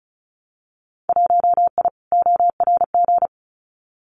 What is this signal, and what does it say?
Morse code, heard as a keyed tone, about 36 wpm, 700 Hz. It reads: 1I ORG
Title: Winter Field Day Exchanges Encore 35wpm